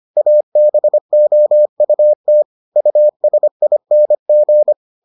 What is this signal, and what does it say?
Morse code, heard as a keyed tone, about 25 wpm, 600 Hz. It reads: ABOUT USING